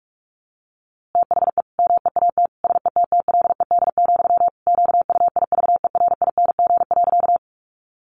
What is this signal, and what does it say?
Morse code, heard as a keyed tone, about 44 wpm, 700 Hz. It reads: THE MEAT SETTLED, QUIVERING.
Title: A Sound Of Thunder 45wpm